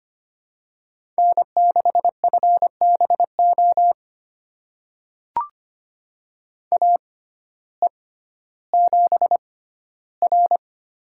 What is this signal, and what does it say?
Morse code, heard as a keyed tone, about 25 wpm, spaced as for 14 wpm, 700 Hz. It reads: N6FBO AE7R
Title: USA Call Signs - 6x Character Spacing - 25wpm